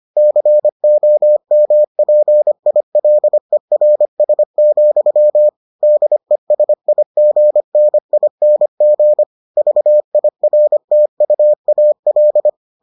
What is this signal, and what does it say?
Morse code, heard as a keyed tone, about 25 wpm, 600 Hz. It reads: COMPILERS, DESIGNING VIRTUAL